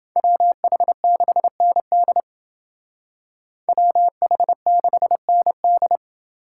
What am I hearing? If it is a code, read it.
Morse code, heard as a keyed tone, about 28 wpm, 700 Hz. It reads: WH6ND WH6ND